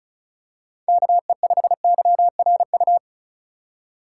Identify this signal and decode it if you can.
Morse code, heard as a keyed tone, about 34 wpm, 700 Hz. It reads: KE5YRU
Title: USA Call Signs - 6 Characters - 3x Character Spacing - 35wpm